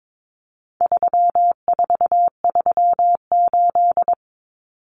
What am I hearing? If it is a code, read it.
Morse code, heard as a keyed tone, about 22 wpm, 700 Hz. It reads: 3438
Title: Numbers - 4-Digit 22wpm